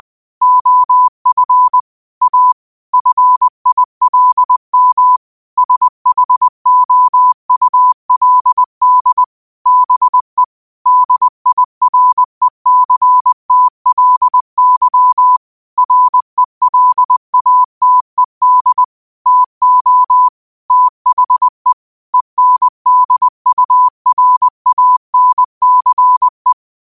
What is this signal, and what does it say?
Morse code, heard as a keyed tone, about 20 wpm, 1000 Hz. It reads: OF A FILM SHOULD BE DIRECTLY RELATED TO THE ENDURANCE
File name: qotdfiles / 20wpm